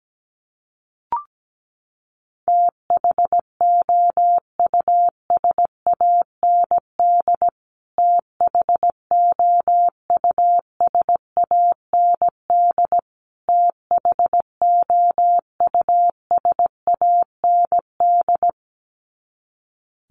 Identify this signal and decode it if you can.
Morse code, heard as a keyed tone, about 17 wpm, 700 Hz. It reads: THOUSAND THOUSAND THOUSAND
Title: Top 500 Words Repeated 17wpm